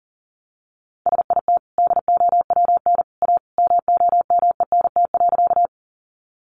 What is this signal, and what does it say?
Morse code, heard as a keyed tone, about 37 wpm, 700 Hz. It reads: SIT DOWN A MOMENT.
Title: Sentences from Top 1000 Words 40wpm